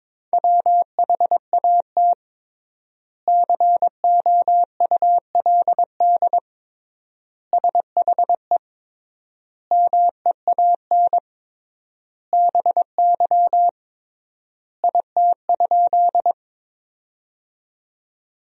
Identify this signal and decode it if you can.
Morse code, heard as a keyed tone, about 22 wpm, 700 Hz. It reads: WHAT COULD SHE MEAN BY IT?
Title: 3x Word-Spacing - Sentences from Top 200 Words - 22wpm